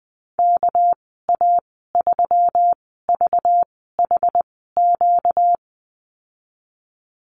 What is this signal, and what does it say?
Morse code, heard as a keyed tone, about 20 wpm, spaced as for 14 wpm, 700 Hz. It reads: KA3VHQ
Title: 245 Lesson 39g - Letter z - Call Signs 20wpm